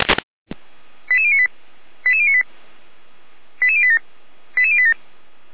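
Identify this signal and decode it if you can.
Five-tone selective call (ZVEI), heard as quick runs of five tones, tones of about 70 ms. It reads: 88098 88098 88087 88087
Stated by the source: ZVEI five-tone call 88086 received by SDR.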